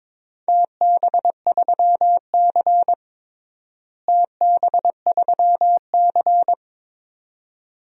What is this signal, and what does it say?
Morse code, heard as a keyed tone, about 22 wpm, 700 Hz. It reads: TB3C TB3C
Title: DXCC-100 Call Signs - 4 Characters - Repeated 2x - 22wpm